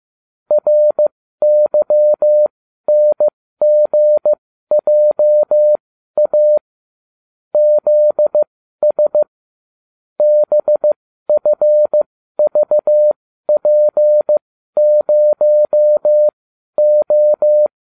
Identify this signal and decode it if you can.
Morse code, heard as a keyed tone, about 15 wpm, spaced as for 12 wpm, 600 Hz. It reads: RYNGJA ZS BFVP0O